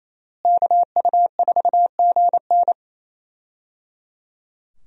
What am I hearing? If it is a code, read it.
Morse code, heard as a keyed tone, about 28 wpm, 700 Hz. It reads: KU4GN
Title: USA Call Signs - 5 Characters - Repeated 3x - 28wpm